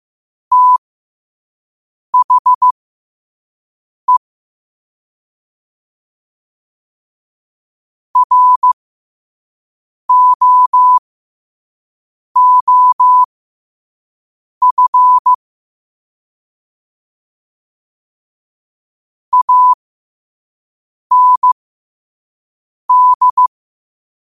Morse code, heard as a keyed tone, about 15 wpm, spaced as for 5 wpm, 1000 Hz. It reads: THE ROOF AND
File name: qotdfiles / 5wpm